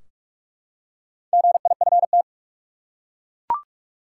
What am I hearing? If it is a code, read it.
Morse code, heard as a keyed tone, about 42 wpm, 700 Hz. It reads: GIFT
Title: Four Letter Words - 45wpm